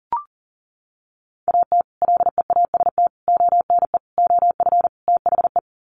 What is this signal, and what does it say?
Morse code, heard as a keyed tone, about 40 wpm, 700 Hz. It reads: AT LEAST ONE OF THE
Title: Sets of 5 Words Encore 40wpm